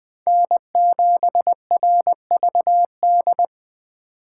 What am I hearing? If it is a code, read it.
Morse code, heard as a keyed tone, about 20 wpm, 700 Hz. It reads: N7RVD
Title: K1USN SST QSO Practice - 20wpm